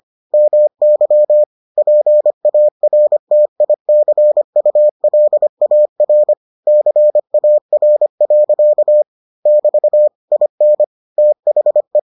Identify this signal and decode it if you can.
Morse code, heard as a keyed tone, about 25 wpm, 600 Hz. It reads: MY PARTICULAR CAR. <BT> IN THE